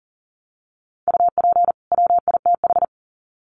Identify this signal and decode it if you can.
Morse code, heard as a keyed tone, about 40 wpm, 700 Hz. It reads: UP WITH
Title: Sets of 2 Words 40wpm